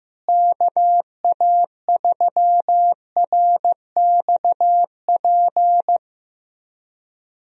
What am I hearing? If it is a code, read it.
Morse code, heard as a keyed tone, about 15 wpm, 700 Hz. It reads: KA3RXP
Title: USA Call Signs - 6 Characters - Repeated 3x - 15wpm